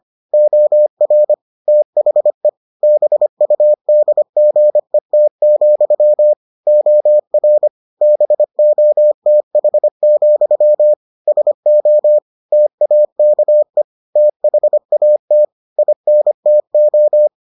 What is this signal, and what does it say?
Morse code, heard as a keyed tone, about 25 wpm, 600 Hz. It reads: OR THE BUDGET, OR BOTH, SO TAKE THAT INTO